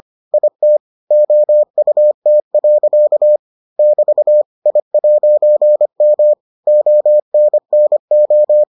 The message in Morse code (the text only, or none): IT OUT. <BT> I'M ONNO